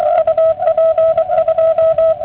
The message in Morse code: UP 2